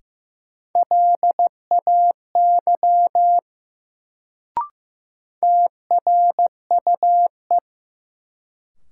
Morse code, heard as a keyed tone, about 15 wpm, 700 Hz. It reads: LAY TRUE
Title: Sentences from Top 400 Words - New Word Drill - 15wpm